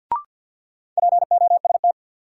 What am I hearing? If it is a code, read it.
Morse code, heard as a keyed tone, about 53 wpm, 700 Hz. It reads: POST
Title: Sentences from Top 1000 Words - New Word Drill - 50wpm